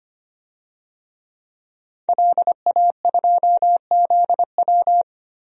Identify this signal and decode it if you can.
Morse code, heard as a keyed tone, about 25 wpm, 700 Hz. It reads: LA2ZW
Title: DXCC-100 Call Signs - 5 Characters - Repeated 2x - 25wpm